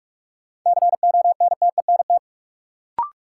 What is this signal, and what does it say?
Morse code, heard as a keyed tone, about 46 wpm, 700 Hz. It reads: CONTENT
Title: Sentences from Top 900 Words - New Word Drill - 45wpm